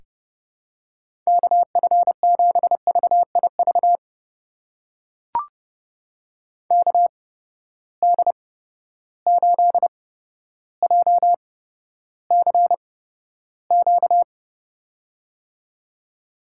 Morse code, heard as a keyed tone, about 30 wpm, spaced as for 13 wpm, 700 Hz. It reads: KF7VIV KD8JCQ
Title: USA Call Signs - 6 Characters - 8x Character Spacing - 30wpm